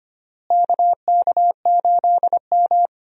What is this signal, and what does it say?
Morse code, heard as a keyed tone, about 25 wpm, 700 Hz. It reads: KK8M
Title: USA Call Signs - 4 Characters - Repeated 3x - 25wpm